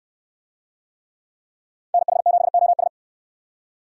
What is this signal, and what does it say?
Morse code, heard as a keyed tone, about 69 wpm, 700 Hz. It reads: NH7OS